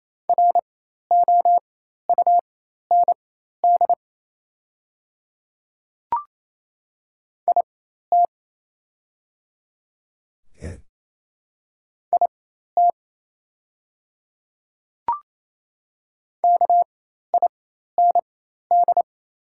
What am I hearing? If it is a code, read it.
Morse code, heard as a keyed tone, about 28 wpm, spaced as for 13 wpm, 690 Hz. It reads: ROUND IT IT KIND
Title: 4x Character Spacing - Top 200 Words - 28wpm